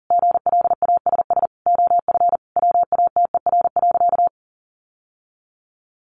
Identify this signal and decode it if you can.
Morse code, heard as a keyed tone, about 37 wpm, 700 Hz. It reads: GLASS OF WATER.
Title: Sentences from Top 1300 Words 40wpm